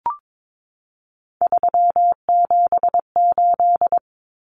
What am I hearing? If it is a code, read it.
Morse code, heard as a keyed tone, about 22 wpm, 700 Hz. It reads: 378
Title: Common Cut Numbers - 3-Digit 22wpm